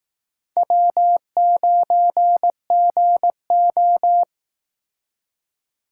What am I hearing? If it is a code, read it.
Morse code, heard as a keyed tone, about 18 wpm, 700 Hz. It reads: W9GO
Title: Speed Racing - USA 1x2 Call Signs 15wpm